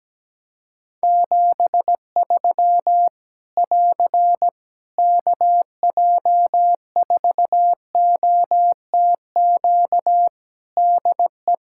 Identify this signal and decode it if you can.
Morse code, heard as a keyed tone, about 17 wpm, 700 Hz. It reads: 73 <AR> KJ4OTQ DE
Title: Practice QSO - Exchange 4 - Ending - 17wpm